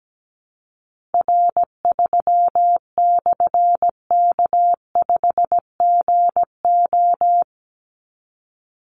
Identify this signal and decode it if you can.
Morse code, heard as a keyed tone, about 17 wpm, 700 Hz. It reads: R3/K5GO
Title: Call Signs Repeated 3x - 17wpm